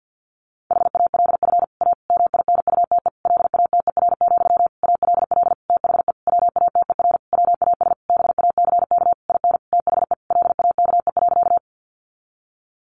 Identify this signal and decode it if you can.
Morse code, heard as a keyed tone, about 50 wpm, 700 Hz. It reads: HALF A MINUTE LATER, ALL THE WATER WAS BACK IN THE LAKE.
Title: Sentences from Top 1100 Words 50wpm